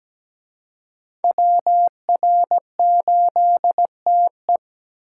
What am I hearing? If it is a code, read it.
Morse code, heard as a keyed tone, about 20 wpm, 700 Hz. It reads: WR8TE
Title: Speed Racing - USA 2x2 Call Signs 17wpm